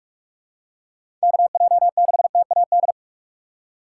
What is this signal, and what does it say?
Morse code, heard as a keyed tone, about 45 wpm, 700 Hz. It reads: KJ6TAD
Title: USA Call Signs - 4x Character Spacing - 45wpm